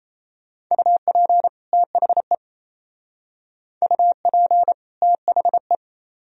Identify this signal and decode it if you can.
Morse code, heard as a keyed tone, about 35 wpm, 700 Hz. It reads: UP THE UP THE
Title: Speed-Racing - Sets of 2 Words 28wpm